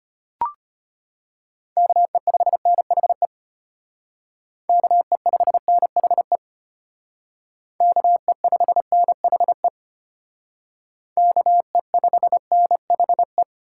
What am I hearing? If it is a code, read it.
Morse code, heard as a keyed tone, about 32 wpm, 700 Hz. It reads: KE5NHE KE5NHE KE5NHE KE5NHE